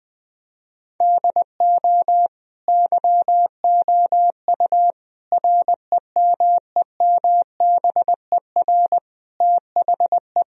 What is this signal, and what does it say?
Morse code, heard as a keyed tone, about 21 wpm, 700 Hz. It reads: DO YOU REMEMBER THE
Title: Sentences from Top 1000 Words - Review Part 1 - 20wpm